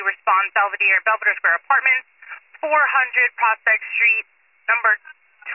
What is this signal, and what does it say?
Emergency Alert System (SAME) tones, heard as harsh buzzing bursts, with speech. It is heard from 1.1 to 5.1 s.